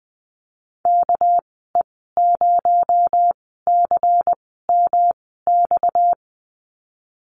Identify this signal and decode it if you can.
Morse code, heard as a keyed tone, about 20 wpm, spaced as for 15 wpm, 700 Hz. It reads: KE0CMX
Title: 231 Lesson 37g - Letter x - Call Signs 20wpm